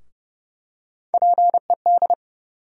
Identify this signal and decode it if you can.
Morse code, heard as a keyed tone, about 30 wpm, 700 Hz. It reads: PED
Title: USA Call Sign Postfixes - 3 Letters 30wpm